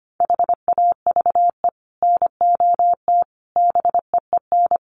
Morse code, heard as a keyed tone, about 25 wpm, 700 Hz. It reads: HAVE NOT BEEN
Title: Sets of 5 Words 25wpm